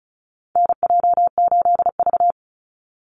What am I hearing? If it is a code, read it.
Morse code, heard as a keyed tone, about 36 wpm, 700 Hz. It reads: NJ8V
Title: Call Signs - 35wpm